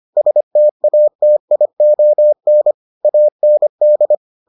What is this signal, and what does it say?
Morse code, heard as a keyed tone, about 25 wpm, 600 Hz. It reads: STATION AND